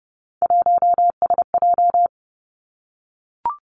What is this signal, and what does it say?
Morse code, heard as a keyed tone, about 30 wpm, 700 Hz. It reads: 1SJ
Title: Three Challenging Letter-Number Combinations 30wpm